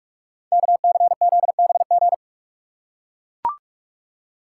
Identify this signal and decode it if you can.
Morse code, heard as a keyed tone, about 44 wpm, 700 Hz. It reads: KCZBG
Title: Code Groups - 5 Letters - 45wpm